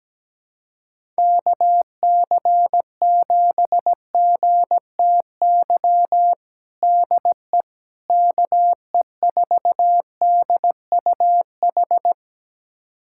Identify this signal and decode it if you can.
Morse code, heard as a keyed tone, about 17 wpm, 700 Hz. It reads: KC7GTY DE KE4DUH
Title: USA Callsign DE Callsign - 6 Characters - 17wpm